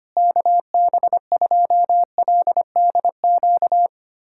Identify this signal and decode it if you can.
Morse code, heard as a keyed tone, about 25 wpm, 700 Hz. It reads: KB2LDQ
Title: USA Callsign DE Callsign - 6 Characters - 25wpm